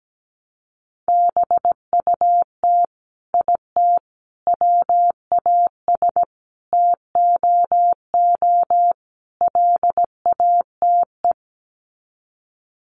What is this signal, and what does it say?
Morse code, heard as a keyed tone, about 17 wpm, 700 Hz. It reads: BUT IT WAS TOO LATE
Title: Sets of 5 Words Encore2 17wpm